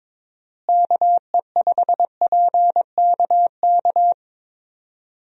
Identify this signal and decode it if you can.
Morse code, heard as a keyed tone, about 22 wpm, 700 Hz. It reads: KE5PKK
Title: USA Call Signs - 6 Characters - Repeated 2x - 22wpm